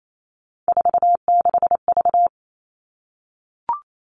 Morse code, heard as a keyed tone, about 28 wpm, 700 Hz. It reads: 46V
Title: All Cut Numbers - 3-Digit 28wpm